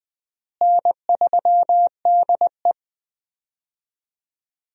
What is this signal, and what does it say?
Morse code, heard as a keyed tone, about 20 wpm, 700 Hz. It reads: N3DE
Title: USA Call Signs - 4 Characters - Repeated 3x - 20wpm